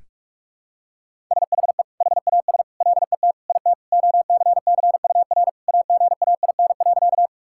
Morse code, heard as a keyed tone, about 41 wpm, 700 Hz. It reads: SHE HAS LET IT OCCUR AGAIN.
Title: Sentences from Top 1000 Words - Review Part 6 - 45wpm